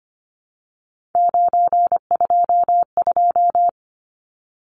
Morse code, heard as a keyed tone, about 25 wpm, 700 Hz. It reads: 922
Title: Numbers - 3-Digit 25wpm